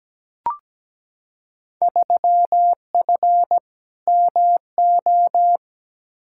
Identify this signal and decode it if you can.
Morse code, heard as a keyed tone, about 17 wpm, 700 Hz. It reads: 3F MO
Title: Field Day Exchanges Encore - 17wpm